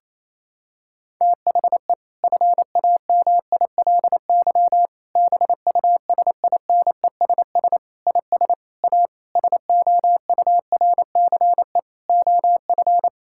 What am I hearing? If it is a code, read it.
Morse code, heard as a keyed tone, about 28 wpm, 700 Hz. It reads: THE FAMILY BUSINESS IS A SOURCE OF
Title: Sentences from Top 1000 Words - Review Part 6 - 28wpm